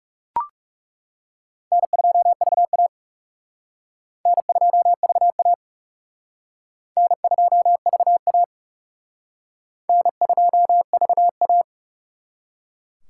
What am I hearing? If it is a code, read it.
Morse code, heard as a keyed tone, about 35 wpm, 700 Hz. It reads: N2VA N2VA N2VA N2VA
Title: Speed Racing - USA 1x2 Call Signs 30wpm